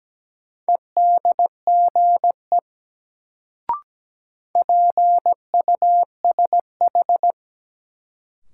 Morse code, heard as a keyed tone, about 17 wpm, 700 Hz. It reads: EDGE PUSH
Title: Sentences from Top 1300 Words - New Word Drill Alt - 17wpm